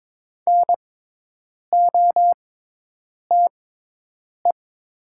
Morse code, heard as a keyed tone, about 22 wpm, spaced as for 8 wpm, 700 Hz. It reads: NOTE
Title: 6x Character Spacing - Top 500 Words - 22wpm